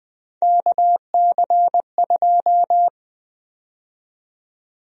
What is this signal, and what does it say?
Morse code, heard as a keyed tone, about 20 wpm, 700 Hz. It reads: KC2
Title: USA Call Sign Prefixes - Repeated 2x - 20wpm